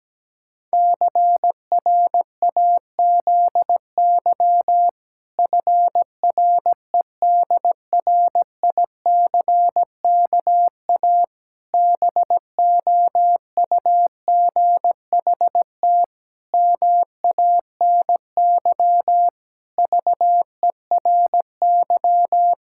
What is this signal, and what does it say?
Morse code, heard as a keyed tone, about 17 wpm, 700 Hz. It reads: CRAZY FREDRICKA BOUGHT MANY VERY
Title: Pangram Challenge v2 - 17wpm